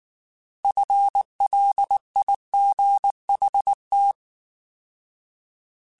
Morse code, heard as a keyed tone, about 19 wpm, 800 Hz. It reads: FLIGHT